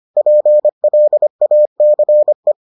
Morse code, heard as a keyed tone, about 25 wpm, 600 Hz. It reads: PLACE